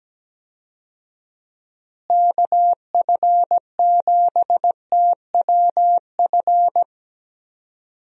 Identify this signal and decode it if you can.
Morse code, heard as a keyed tone, about 17 wpm, 700 Hz. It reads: KF7TWF
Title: USA Call Signs - 6 Characters - Repeated 3x - 17wpm